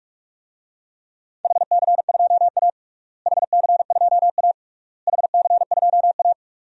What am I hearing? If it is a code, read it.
Morse code, heard as a keyed tone, about 46 wpm, 700 Hz. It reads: HC2A HC2A HC2A
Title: DXCC-100 Call Signs - 4 Characters - Repeated 3x - 45wpm